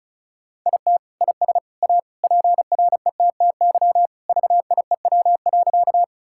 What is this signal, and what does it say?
Morse code, heard as a keyed tone, about 36 wpm, 700 Hz. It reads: IT IS A PRETTY VIEW.
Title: Sentences from Top 1000 Words - Review Part 6 - 35wpm